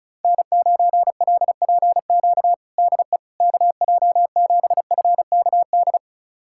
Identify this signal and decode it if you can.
Morse code, heard as a keyed tone, about 36 wpm, 700 Hz. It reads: N9LPQ DE KJ7FKD
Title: USA Callsign DE Callsign 35wpm